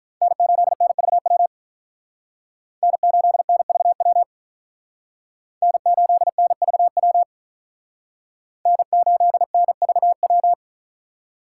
Morse code, heard as a keyed tone, about 45 wpm, 700 Hz. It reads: N8NVW N8NVW N8NVW N8NVW